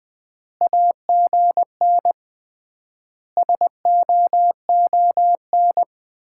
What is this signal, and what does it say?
Morse code, heard as a keyed tone, about 20 wpm, 700 Hz. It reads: AGN SOON
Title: Common QSO Phrases - 3x Word-Spacing - 20wpm